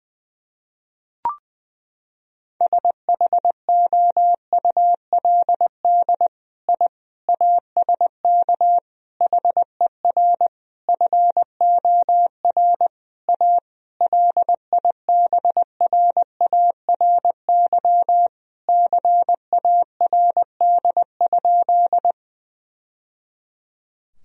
Morse code, heard as a keyed tone, about 20 wpm, 700 Hz. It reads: SHOULD I ASK HER FOR A LIBRARY CARD?
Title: Sentences from Top 1000 Words - Review Part 4 - 20wpm